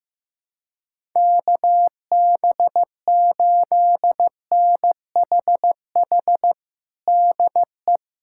Morse code, heard as a keyed tone, about 15 wpm, 700 Hz. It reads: KB8NHH DE
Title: USA Callsign DE Callsign 15wpm